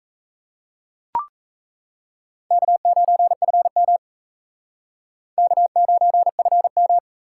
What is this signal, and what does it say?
Morse code, heard as a keyed tone, about 35 wpm, 700 Hz. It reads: K9FM K9FM